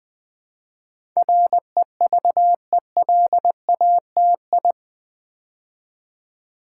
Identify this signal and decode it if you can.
Morse code, heard as a keyed tone, about 20 wpm, 700 Hz. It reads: REVELATI
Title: Build A Word - 4 Letters At A Time - 10-Letter Words - 20wpm